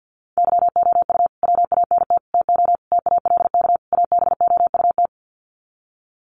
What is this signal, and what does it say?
Morse code, heard as a keyed tone, about 51 wpm, 700 Hz. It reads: YOU WANT TO TALK ABOUT
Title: Sets of 5 Words Encore 50wpm